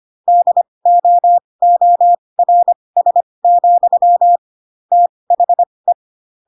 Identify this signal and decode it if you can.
Morse code, heard as a keyed tone, about 25 wpm, spaced as for 20 wpm, 700 Hz. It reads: DOORS, THE
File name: Pinocchio / Pinocchio ch6 15 25 ik1hge